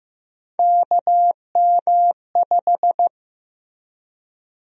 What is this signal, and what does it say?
Morse code, heard as a keyed tone, about 16 wpm, 700 Hz. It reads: KM5
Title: USA Call Sign Prefixes - Repeated 2x - 15wpm